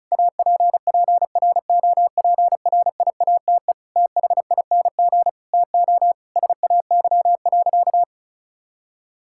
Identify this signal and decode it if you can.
Morse code, heard as a keyed tone, about 36 wpm, 690 Hz. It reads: APPROPRIATE THING TO SAY.
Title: Sentences from Top 1000 Words - Review Part 5 - 35wpm